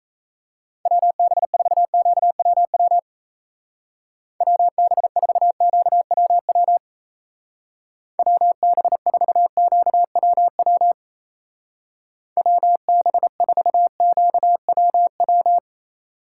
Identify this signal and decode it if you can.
Morse code, heard as a keyed tone, about 35 wpm, 700 Hz. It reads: WB4QWW WB4QWW WB4QWW WB4QWW